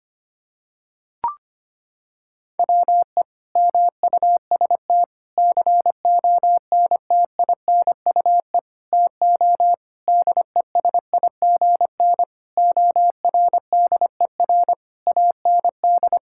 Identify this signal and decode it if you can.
Morse code, heard as a keyed tone, about 25 wpm, 700 Hz. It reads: WE MUST CONTINUE TO DESIGN ORDER AND
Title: Sentences from Top 1000 Words - Review Part 6 - 25wpm